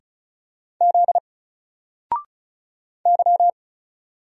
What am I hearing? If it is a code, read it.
Morse code, heard as a keyed tone, about 36 wpm, 700 Hz. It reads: Z Y
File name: Single-Letter-Number-35wpm